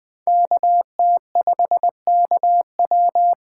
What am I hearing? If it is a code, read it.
Morse code, heard as a keyed tone, about 21 wpm, 700 Hz. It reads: KT5KW